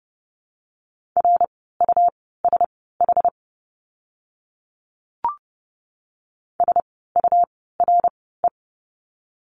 Morse code, heard as a keyed tone, about 30 wpm, spaced as for 17 wpm, 700 Hz. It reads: RUSH SURE
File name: Lesson 17f - Letter u - Words 30wpm